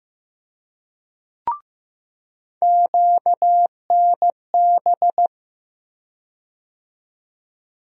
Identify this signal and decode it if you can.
Morse code, heard as a keyed tone, about 15 wpm, 700 Hz. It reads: QNB
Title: USA Call Sign Postfixes - 3 Letters 15wpm